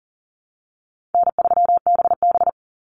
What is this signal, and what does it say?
Morse code, heard as a keyed tone, about 40 wpm, 700 Hz. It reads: N3BB
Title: Easy Call Signs Repeated 3x - 40wpm